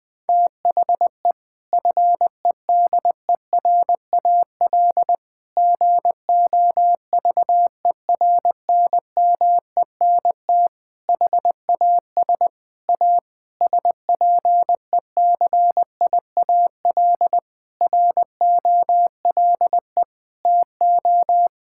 Morse code, heard as a keyed tone, about 20 wpm, 700 Hz. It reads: THE FEDERAL GOVERNMENT HAS A SPECIAL ROLE TO
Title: Sentences from Top 1000 Words - Review Part 6 - 20wpm